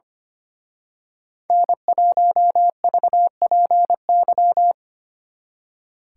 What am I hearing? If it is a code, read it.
Morse code, heard as a keyed tone, about 29 wpm, 700 Hz. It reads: N1VPY
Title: Speed Racing - USA 1x3 Call Signs 25wpm